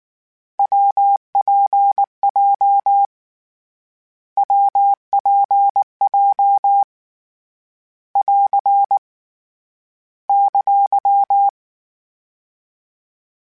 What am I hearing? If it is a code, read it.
Morse code, heard as a keyed tone, about 19 wpm, 800 Hz. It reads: WPJ WPJ <AR> !
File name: Variable Speed / Variable Speed-mp3s / WPJ WPJ1 sp @18